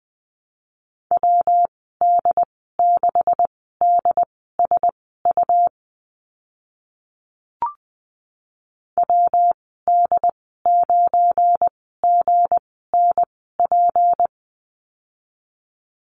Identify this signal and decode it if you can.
Morse code, heard as a keyed tone, about 20 wpm, spaced as for 14 wpm, 700 Hz. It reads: WD6DSU WD9GNP
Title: 173 Lesson 28g - Number 9 - Call Signs 20wpm